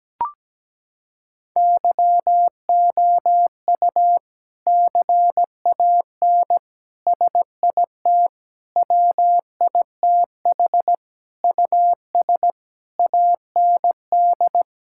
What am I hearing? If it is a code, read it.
Morse code, heard as a keyed tone, about 17 wpm, 700 Hz. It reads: YOU CAN SIT WITH US AND
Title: Sentences from Top 1000 Words - Review Part 5 - 17wpm